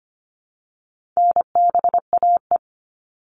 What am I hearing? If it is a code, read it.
Morse code, heard as a keyed tone, about 25 wpm, 700 Hz. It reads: NBAE
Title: All Cut Numbers - 4-Digit 25wpm